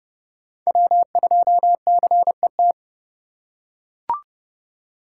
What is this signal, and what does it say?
Morse code, heard as a keyed tone, about 31 wpm, 700 Hz. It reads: W2CET
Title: USA Call Signs - 5 Characters - 4x Character Spacing - 30wpm